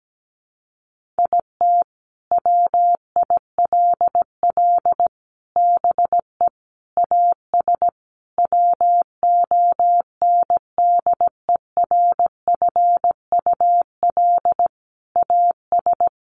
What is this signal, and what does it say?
Morse code, heard as a keyed tone, about 17 wpm, 700 Hz. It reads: IT WILL BE AS WONDERFUL AS
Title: Sentences from Top 1400 Words 17wpm